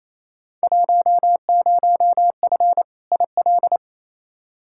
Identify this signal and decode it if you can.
Morse code, heard as a keyed tone, about 28 wpm, 700 Hz. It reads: 10F IL
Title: Field Day Exchanges Encore - 28wpm